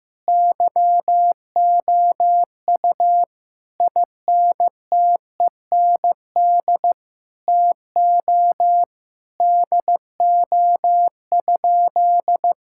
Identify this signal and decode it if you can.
Morse code, heard as a keyed tone, about 16 wpm, 700 Hz. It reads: YOU INTEND TO DO?
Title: Sentences from Top 1600 Words 15wpm